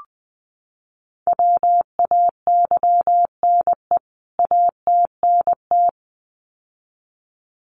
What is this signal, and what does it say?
Morse code, heard as a keyed tone, about 20 wpm, 700 Hz. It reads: WAYNE ATNT
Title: CWT Exchange 20wpm